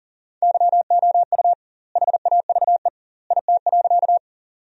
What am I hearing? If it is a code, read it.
Morse code, heard as a keyed tone, about 40 wpm, 700 Hz. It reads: YOU HAVE IT.
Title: 2x Word-Spacing - Sentences from Top 100 Words - 40wpm